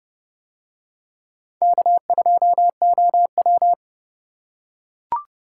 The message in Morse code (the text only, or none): K2OW